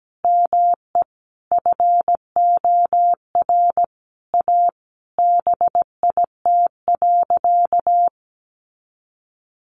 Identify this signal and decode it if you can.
Morse code, heard as a keyed tone, about 17 wpm, 700 Hz. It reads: ME FOR A BIT.
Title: Sentences from Top 900 Words 17wpm